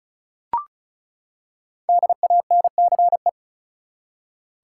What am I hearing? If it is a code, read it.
Morse code, heard as a keyed tone, about 34 wpm, 700 Hz. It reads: DANCE
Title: Five Letter Words - 35wpm